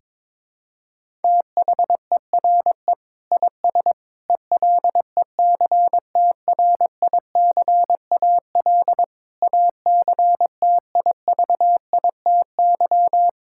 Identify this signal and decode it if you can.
Morse code, heard as a keyed tone, about 22 wpm, 700 Hz. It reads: THERE IS ELECTRICAL ACTIVITY
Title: Sentences from Top 1000 Words - Review Part 6 - 22wpm